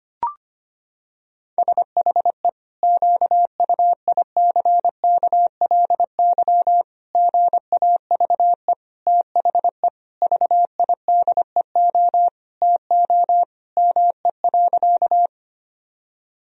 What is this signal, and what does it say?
Morse code, heard as a keyed tone, about 25 wpm, 700 Hz. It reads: SHE QUICKLY GAVE THE VIDEO TO ME.
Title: Sentences from Top 1000 Words - Review Part 4 - 25wpm